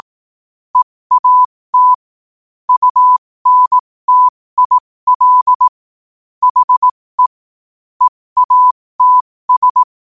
Morse code, heard as a keyed tone, about 18 wpm, spaced as for 15 wpm, 1000 Hz. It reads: EAT UNTIL HE EATS
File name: qotdfiles / 15wpm